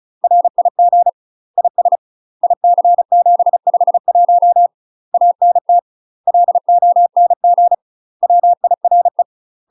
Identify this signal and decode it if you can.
Morse code, heard as a keyed tone, about 34 wpm, 700 Hz. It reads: RIG IS IC751 ANT LONG WIRE
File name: UK MASTER QSOs TRAINING / 35WPM / UK MASTER QSOs 0003